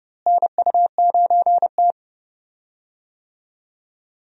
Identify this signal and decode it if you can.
Morse code, heard as a keyed tone, about 32 wpm, 700 Hz. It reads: NU9T
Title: USA Call Signs - 4 Characters - Repeated 2x - 30wpm